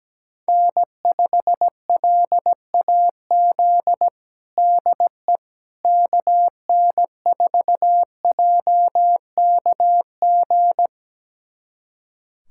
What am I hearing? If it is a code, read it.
Morse code, heard as a keyed tone, about 17 wpm, 700 Hz. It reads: N5LAZ DE KN4JKG
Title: USA Callsign DE Callsign 17wpm